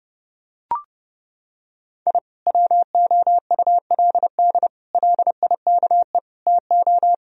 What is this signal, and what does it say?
Morse code, heard as a keyed tone, about 32 wpm, 700 Hz. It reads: I WOULD LIKE TO
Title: Sentences from Top 1000 Words - Review Part 1 - 30wpm